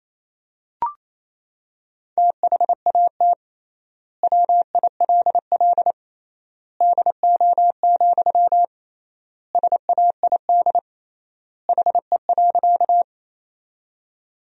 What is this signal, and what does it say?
Morse code, heard as a keyed tone, about 28 wpm, 700 Hz. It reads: THAT WILL DO, SAID HE.
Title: 3x Word-Spacing - Sentences from Top 100 Words - 28wpm